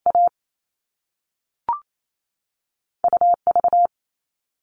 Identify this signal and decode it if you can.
Morse code, heard as a keyed tone, about 28 wpm, 700 Hz. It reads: A UV
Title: All Cut Numbers - 2-Digit 28wpm